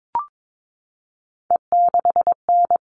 Morse code, heard as a keyed tone, about 22 wpm, 700 Hz. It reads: E6N
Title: All Cut Numbers - 3-Digit 22wpm